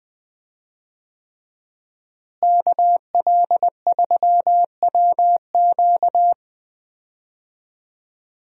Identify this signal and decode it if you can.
Morse code, heard as a keyed tone, about 20 wpm, 700 Hz. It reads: KL3WQ
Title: USA Call Signs - 5 Characters - Repeated 3x - 20wpm